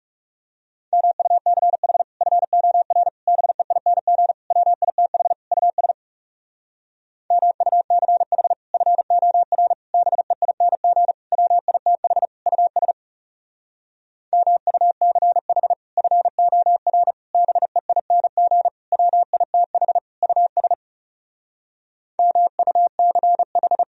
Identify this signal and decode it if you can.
Morse code, heard as a keyed tone, about 40 wpm, 700 Hz. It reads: MUCH FOR BEING WITH US MUCH FOR BEING WITH US MUCH FOR BEING WITH US MUCH